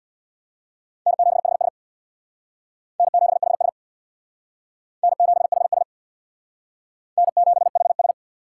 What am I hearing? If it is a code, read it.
Morse code, heard as a keyed tone, about 62 wpm, 700 Hz. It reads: N7HS N7HS N7HS N7HS
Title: Speed Racing - USA 1x2 Call Signs 50wpm